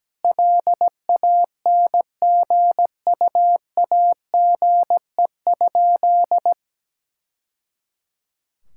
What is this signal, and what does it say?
Morse code, heard as a keyed tone, about 17 wpm, 700 Hz. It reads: LANGUAGE?
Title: Sentences from Top 1000 Words - Review Part 5 - 17wpm